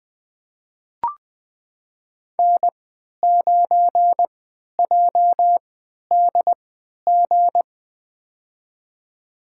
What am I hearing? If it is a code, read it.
Morse code, heard as a keyed tone, about 20 wpm, spaced as for 11 wpm, 700 Hz. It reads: N9JDG
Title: USA Call Signs - 3x Character Spacing - 20wpm